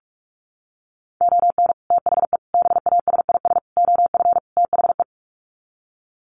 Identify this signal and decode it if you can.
Morse code, heard as a keyed tone, about 44 wpm, 700 Hz. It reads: ON THE BASIS OF THE
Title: Sets of 5 Words 45wpm